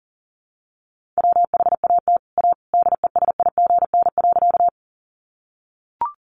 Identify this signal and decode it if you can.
Morse code, heard as a keyed tone, about 37 wpm, 700 Hz. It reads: WHAT A DESIGN.
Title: Sentences from Top 800 Words 40wpm